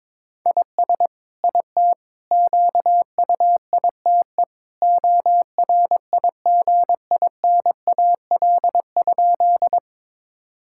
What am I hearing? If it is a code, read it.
Morse code, heard as a keyed tone, about 22 wpm, 700 Hz. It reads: IS IT QUITE ORIGINAL?
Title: Sentences from Top 1000 Words - Review Part 6 - 22wpm